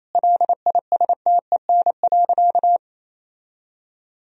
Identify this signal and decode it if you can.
Morse code, heard as a keyed tone, about 28 wpm, 700 Hz. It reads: LISTEN.
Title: Sentences from Top 1000 Words - Review Part 3 - 28wpm